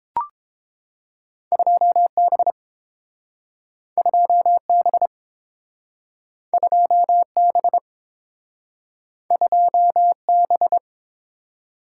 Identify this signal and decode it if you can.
Morse code, heard as a keyed tone, about 30 wpm, 700 Hz. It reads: 2B 2B 2B 2B